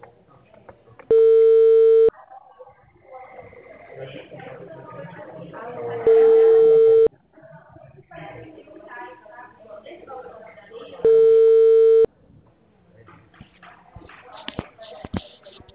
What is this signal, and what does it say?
Ringback tone (China, 450 Hz), with speech.